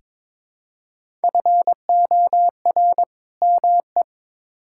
Morse code, heard as a keyed tone, about 23 wpm, 700 Hz. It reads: FOR ME
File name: Build A Phrase - 2-Word Phrases - 22wpm